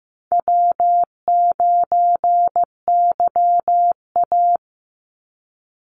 Morse code, heard as a keyed tone, about 15 wpm, 700 Hz. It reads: W9YA
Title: Easy Call Signs - 15wpm